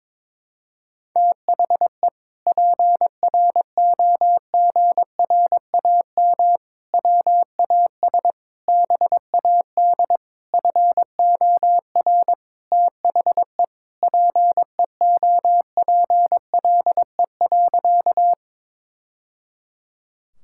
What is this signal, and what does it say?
Morse code, heard as a keyed tone, about 22 wpm, 700 Hz. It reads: THE PROGRAM WAS BAD FOR THE PEOPLE.
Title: Sentences from Top 1000 Words - Review Part 5 - 22wpm